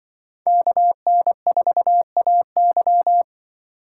Morse code, heard as a keyed tone, about 23 wpm, 700 Hz. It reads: KN4AY
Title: Speed Racing - USA 2x2 Call Signs 20wpm